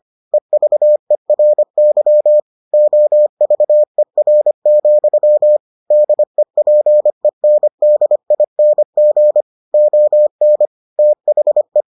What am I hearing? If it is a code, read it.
Morse code, heard as a keyed tone, about 25 wpm, 600 Hz. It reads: EVERY OVER, DEPENDING ON THE